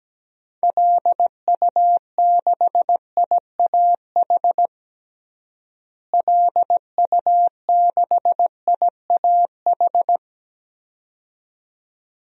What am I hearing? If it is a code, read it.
Morse code, heard as a keyed tone, about 17 wpm, 700 Hz. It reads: LU6IAH LU6IAH
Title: DXCC-100 Call Signs - 6 Characters - Repeated 2x - 17wpm